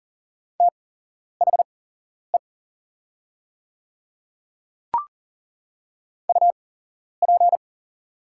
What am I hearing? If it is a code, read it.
Morse code, heard as a keyed tone, about 39 wpm, spaced as for 11 wpm, 700 Hz. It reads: THE UP
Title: 8x Character Spacing - Top 100 Words - 40wpm